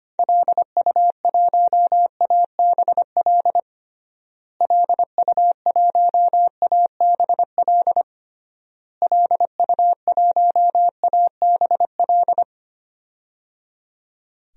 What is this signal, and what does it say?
Morse code, heard as a keyed tone, about 25 wpm, 700 Hz. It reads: LU1ABL LU1ABL LU1ABL
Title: DXCC-100 Call Signs - 6 Characters - Repeated 3x - 25wpm